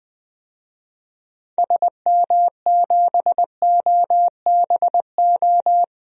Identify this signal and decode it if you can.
Morse code, heard as a keyed tone, about 20 wpm, 700 Hz. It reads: SM7OBO
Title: DXCC-100 Call Signs - 6 Characters - Repeated 3x - 20wpm